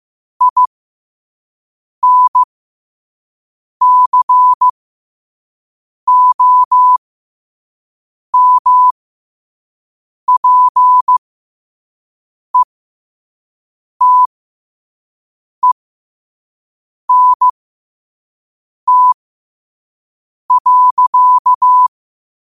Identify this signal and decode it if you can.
Morse code, heard as a keyed tone, about 15 wpm, spaced as for 5 wpm, 1000 Hz. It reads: INCOMPETENT.
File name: qotdfiles / 5wpm